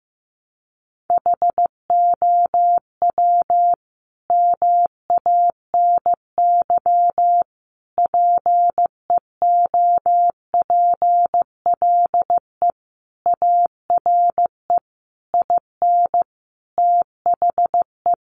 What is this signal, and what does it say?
Morse code, heard as a keyed tone, about 15 wpm, 700 Hz. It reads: HOW MANY PEOPLE ARE IN THE
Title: Sentences from Top 1200 Words 15wpm